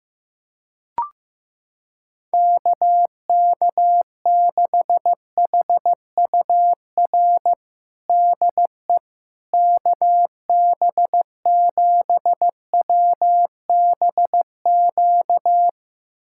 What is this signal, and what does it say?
Morse code, heard as a keyed tone, about 15 wpm, 700 Hz. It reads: KK6HUR DE KB7WBQ
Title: USA Callsign DE Callsign - 6 Characters - 15wpm